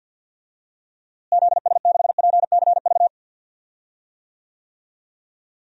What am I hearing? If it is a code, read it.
Morse code, heard as a keyed tone, about 48 wpm, 700 Hz. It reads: ZS6PCV
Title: DXCC-100 Call Signs - 6 Characters - Repeated 2x - 50wpm